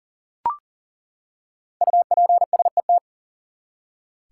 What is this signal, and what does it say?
Morse code, heard as a keyed tone, about 40 wpm, 700 Hz. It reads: UPSET
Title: Five Letter Words - 40wpm